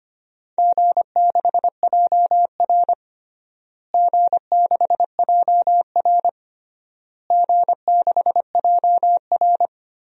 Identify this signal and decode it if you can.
Morse code, heard as a keyed tone, about 25 wpm, 700 Hz. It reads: G6JR G6JR G6JR
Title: DXCC-100 Call Signs - 4 Characters - Repeated 3x - 25wpm